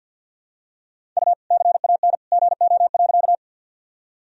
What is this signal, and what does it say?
Morse code, heard as a keyed tone, about 50 wpm, 700 Hz. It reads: U CAN GO.
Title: Sentences from Top 100 Words and Abbreviations 50wpm